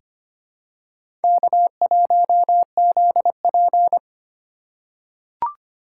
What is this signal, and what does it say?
Morse code, heard as a keyed tone, about 25 wpm, 700 Hz. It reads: K1ZP
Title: USA Call Signs - 4 Characters - 2x Character Spacing - 25wpm